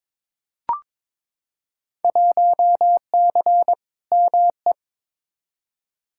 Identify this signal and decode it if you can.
Morse code, heard as a keyed tone, about 22 wpm, 700 Hz. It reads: 1C ME
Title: Field Day Exchanges - 22wpm